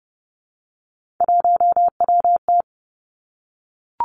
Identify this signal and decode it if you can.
Morse code, heard as a keyed tone, about 30 wpm, 700 Hz. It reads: 1WT
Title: Three Challenging Letter-Number Combinations 30wpm